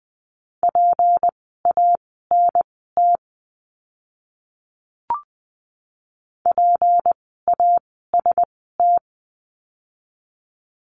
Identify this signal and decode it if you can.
Morse code, heard as a keyed tone, about 20 wpm, spaced as for 15 wpm, 700 Hz. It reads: PANT PAST
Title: 153 Lesson 25f - Letter p - Words 20wpm